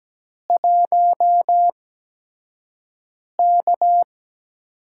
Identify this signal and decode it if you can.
Morse code, heard as a keyed tone, about 17 wpm, spaced as for 5 wpm, 700 Hz. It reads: 1K
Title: USA Call Signs - 4 Characters - 8x Character Spacing - 17wpm